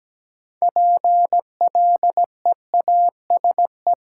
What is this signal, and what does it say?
Morse code, heard as a keyed tone, about 17 wpm, 700 Hz. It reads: PLEASE
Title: Sentences from Top 1500 Words - New Word Drill - 17wpm